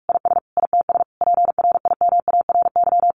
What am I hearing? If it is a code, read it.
Morse code, heard as a keyed tone, about 44 wpm, 700 Hz. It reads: IS ITS PRIMARY
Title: Sentences from Top 1100 Words 45wpm